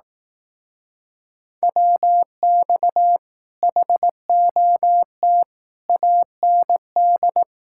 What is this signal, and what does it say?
Morse code, heard as a keyed tone, about 18 wpm, 700 Hz. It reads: WX HOT AND
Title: Speed-Racing Common QSO Phrases Encore - 15wpm